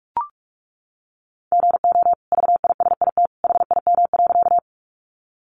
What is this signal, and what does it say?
Morse code, heard as a keyed tone, about 44 wpm, 700 Hz. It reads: GO VISIT HIM.
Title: Sentences from Top 900 Words 45wpm